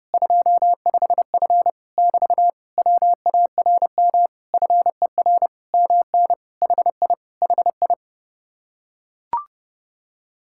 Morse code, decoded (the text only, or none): 25F <BT> WARM FER MN HI HI